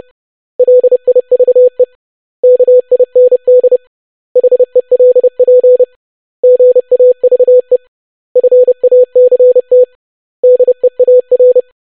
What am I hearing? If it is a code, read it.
Morse code, heard as a keyed tone, about 30 wpm, 500 Hz. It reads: LIVE KIND HELP GAVE FACT DEAR
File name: basic500-25wpm